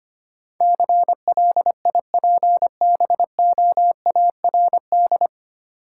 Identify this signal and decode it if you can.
Morse code, heard as a keyed tone, about 25 wpm, 690 Hz. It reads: CLIPBOARD
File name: practice / 25wpm / Easy Words with Hints - 25wpm